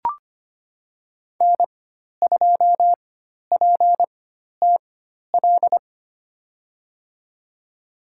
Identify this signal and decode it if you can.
Morse code, heard as a keyed tone, about 25 wpm, spaced as for 12 wpm, 700 Hz. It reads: N2PTL
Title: USA Call Signs - 5 Characters - 4x Character Spacing - 25wpm